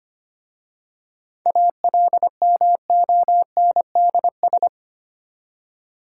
Morse code, heard as a keyed tone, about 25 wpm, 700 Hz. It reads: ALMONDS
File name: practice / 25wpm / Difficult Words with Hints - 25wpm